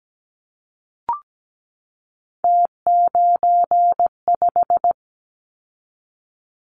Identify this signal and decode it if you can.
Morse code, heard as a keyed tone, about 17 wpm, 700 Hz. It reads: T95
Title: Callsign Prefixes for Top 50 DXCC - 17wpm